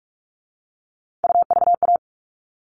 Morse code, heard as a keyed tone, about 44 wpm, 700 Hz. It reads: UVA
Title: All Cut Numbers - 3-Digit 45wpm